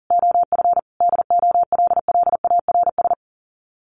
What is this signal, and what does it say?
Morse code, heard as a keyed tone, about 37 wpm, 700 Hz. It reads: OF DOLLARS
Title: Sets of 5 Words Encore 40wpm